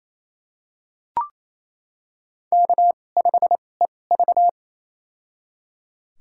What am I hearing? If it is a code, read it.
Morse code, heard as a keyed tone, about 27 wpm, spaced as for 20 wpm, 700 Hz. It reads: K5EV
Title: USA Call Signs - 4 Characters - 2x Character Spacing - 28wpm